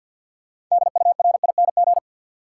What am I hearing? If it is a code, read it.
Morse code, heard as a keyed tone, about 53 wpm, 700 Hz. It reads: DURING
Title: Sets of 2 Words - New Word Drill - 50wpm